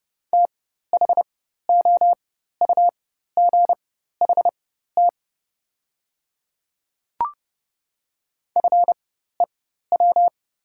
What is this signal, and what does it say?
Morse code, heard as a keyed tone, about 30 wpm, spaced as for 14 wpm, 700 Hz. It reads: THOUGHT FEW
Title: 4x Character Spacing - Top 500 Words - 30wpm